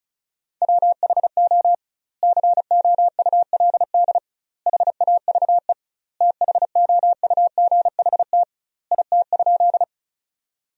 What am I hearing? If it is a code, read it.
Morse code, heard as a keyed tone, about 36 wpm, 700 Hz. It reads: WHO COULD HAVE THOUGHT IT?
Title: 2x Word-Spacing - Sentences from Top 300 Words - 35wpm